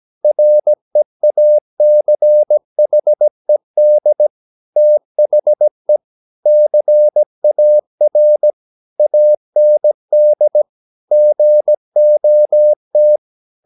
Morse code, heard as a keyed tone, about 17 wpm, 600 Hz. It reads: REACHED THE CAR AND GOT
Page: Romanen Never af Ken Follet på engelsk.